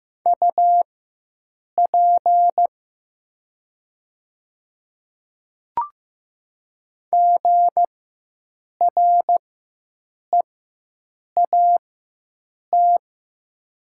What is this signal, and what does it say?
Morse code, heard as a keyed tone, about 15 wpm, spaced as for 7 wpm, 700 Hz. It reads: UP GREAT
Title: 4x Character Spacing - Top 200 Words - 15wpm